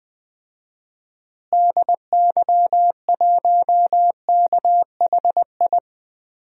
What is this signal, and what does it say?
Morse code, heard as a keyed tone, about 21 wpm, 700 Hz. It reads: DY1KHI
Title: DXCC-100 Call Signs - 6 Characters - Repeated 2x - 20wpm